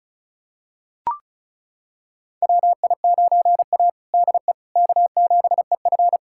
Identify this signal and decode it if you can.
Morse code, heard as a keyed tone, about 36 wpm, 700 Hz. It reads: WI9A DE K7EF
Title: USA Callsign DE Callsign 35wpm